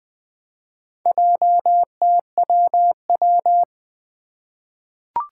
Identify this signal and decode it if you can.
Morse code, heard as a keyed tone, about 20 wpm, 700 Hz. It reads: JTWW
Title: Four Challenging Letter-Number Combinations 20wpm